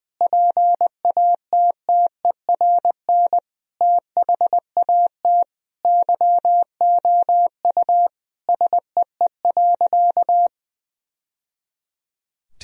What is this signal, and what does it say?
Morse code, heard as a keyed tone, about 20 wpm, 700 Hz. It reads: PATTERN THAT YOU SEE.
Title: Sentences from Top 1000 Words - Review Part 2 - 20wpm